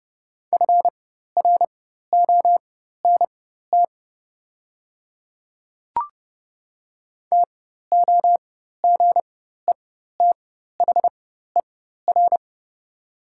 Morse code, heard as a keyed tone, about 30 wpm, spaced as for 14 wpm, 700 Hz. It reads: FRONT TOGETHER
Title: 4x Character Spacing - Top 500 Words - 30wpm